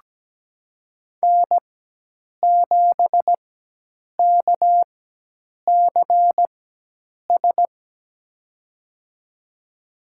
Morse code, heard as a keyed tone, about 17 wpm, spaced as for 8 wpm, 700 Hz. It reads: N7KCS
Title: USA Call Signs - 4x Character Spacing - 17wpm